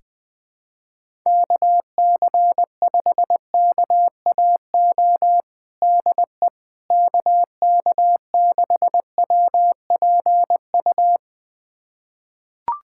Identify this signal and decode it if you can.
Morse code, heard as a keyed tone, about 20 wpm, 700 Hz. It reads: KC5KAO DE KK6WPU
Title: USA Callsign DE Callsign 20wpm